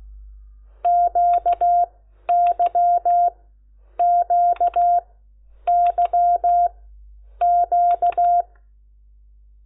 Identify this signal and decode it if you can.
Morse code, heard as a keyed tone, about 16 wpm, 700 Hz. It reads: Q Y Q Y Q
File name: Lecciones